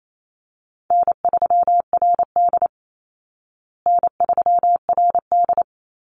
Morse code, heard as a keyed tone, about 28 wpm, 700 Hz. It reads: N3RD N3RD
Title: Easy Call Signs Repeated 2x - 28wpm